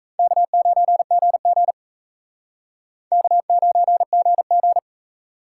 Morse code, heard as a keyed tone, about 34 wpm, 700 Hz. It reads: K9GG K9GG
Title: Speed Racing - USA 1x2 Call Signs 28wpm